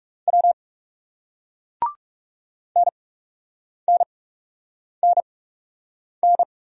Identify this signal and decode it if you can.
Morse code, heard as a keyed tone, about 40 wpm, spaced as for 8 wpm, 700 Hz. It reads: W NNNN